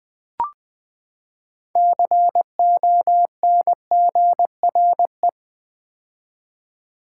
Morse code, heard as a keyed tone, about 20 wpm, 700 Hz. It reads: CONGRE
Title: Build A Word - 6 Letters At A Time - 12-Letter Words - 20wpm